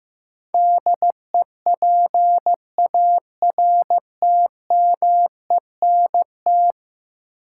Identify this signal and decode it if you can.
Morse code, heard as a keyed tone, about 16 wpm, 700 Hz. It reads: DEPARTMENT
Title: Sentences from Top 1200 Words - New Word Drill Alt - 15wpm